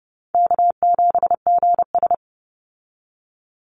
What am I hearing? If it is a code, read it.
Morse code, heard as a keyed tone, about 30 wpm, 700 Hz. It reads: K7GS
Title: Easy Call Signs Repeated 2x - 30wpm